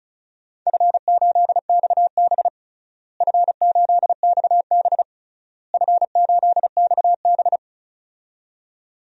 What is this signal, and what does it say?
Morse code, heard as a keyed tone, about 34 wpm, 700 Hz. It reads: F8XB F8XB F8XB
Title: DXCC-100 Call Signs - 4 Characters - Repeated 3x - 35wpm